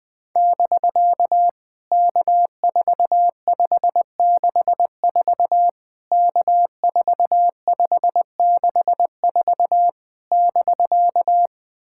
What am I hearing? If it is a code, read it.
Morse code, heard as a keyed tone, about 20 wpm, 700 Hz. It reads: <BK> K4564 K4564 <BK>
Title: POTA QSO Practice - 20wpm